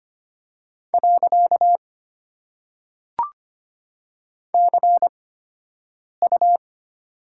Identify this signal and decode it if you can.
Morse code, heard as a keyed tone, about 25 wpm, spaced as for 7 wpm, 700 Hz. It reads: . CU
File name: Top 90 QSO Elements - 8x Character Spacing - 25wpm